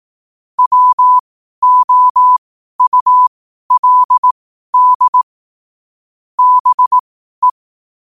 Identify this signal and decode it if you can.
Morse code, heard as a keyed tone, about 18 wpm, spaced as for 12 wpm, 1000 Hz. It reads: WOULD BE
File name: qotdfiles / 12wpm